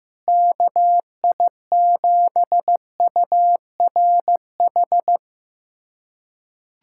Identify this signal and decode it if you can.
Morse code, heard as a keyed tone, about 15 wpm, 700 Hz. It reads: KI7URH
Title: USA Callsign DE Callsign - 6 Characters - 15wpm